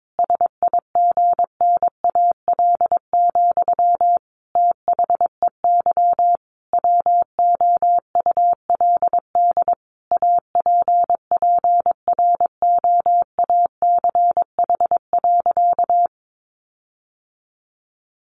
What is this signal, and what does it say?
Morse code, heard as a keyed tone, about 22 wpm, 700 Hz. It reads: SIGNAL, THEY WOULD APPROACH.
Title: Sentences from Top 1400 Words 22wpm